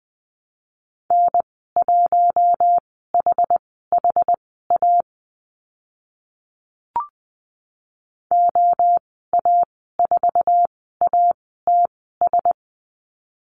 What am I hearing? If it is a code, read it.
Morse code, heard as a keyed tone, about 20 wpm, spaced as for 15 wpm, 700 Hz. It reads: N1HHA OA4ATS
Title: 082 Lesson 14g - Number 2 - Call Signs 20wpm